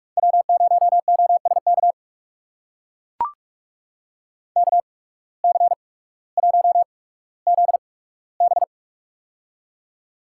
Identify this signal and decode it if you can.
Morse code, heard as a keyed tone, about 44 wpm, spaced as for 19 wpm, 700 Hz. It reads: W0OSK KC1ZB
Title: USA Call Signs - 5 Characters - 8x Character Spacing - 45wpm